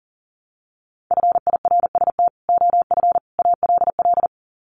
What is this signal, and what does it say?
Morse code, heard as a keyed tone, about 40 wpm, 700 Hz. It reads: FIRST OF ALL
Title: Sets of 3 Words 40wpm